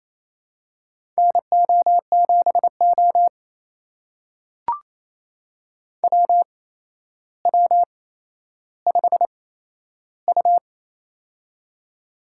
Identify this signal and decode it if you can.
Morse code, heard as a keyed tone, about 27 wpm, spaced as for 12 wpm, 700 Hz. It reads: NO7O WW5U
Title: USA Call Signs - 4 Characters - 8x Character Spacing - 28wpm